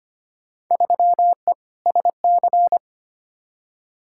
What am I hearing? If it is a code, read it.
Morse code, heard as a keyed tone, about 25 wpm, 700 Hz. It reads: 3E SC
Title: Field Day Exchanges Encore - 25wpm